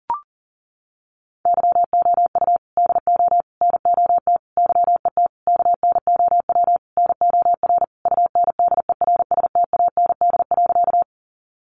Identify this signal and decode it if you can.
Morse code, heard as a keyed tone, about 37 wpm, 700 Hz. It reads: YOU DO NOT YET KNOW NOR UNDERSTAND.
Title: Sentences from Top 1500 Words 40wpm